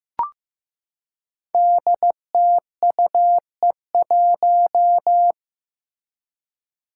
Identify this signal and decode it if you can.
Morse code, heard as a keyed tone, about 15 wpm, 700 Hz. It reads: DTUE1
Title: Five Challenging Letter-Number Combinations 15wpm